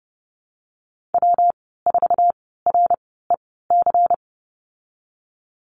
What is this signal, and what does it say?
Morse code, heard as a keyed tone, about 30 wpm, spaced as for 17 wpm, 700 Hz. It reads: W4REC
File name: Lesson 19g - Letter w - Call Signs 30wpm